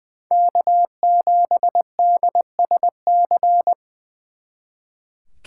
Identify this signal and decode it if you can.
Morse code, heard as a keyed tone, about 23 wpm, 700 Hz. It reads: K7DSC
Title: Speed Racing - USA Call Signs 20wpm